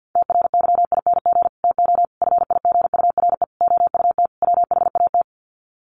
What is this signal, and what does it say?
Morse code, heard as a keyed tone, about 50 wpm, 700 Hz. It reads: TRYING TO FIGURE OUT WHAT
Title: Sets of 5 Words 50wpm